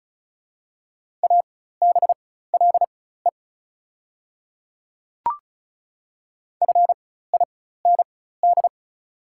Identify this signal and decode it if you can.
Morse code, heard as a keyed tone, about 36 wpm, spaced as for 16 wpm, 690 Hz. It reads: ABLE FIND
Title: 4x Character Spacing - Top 500 Words - 35wpm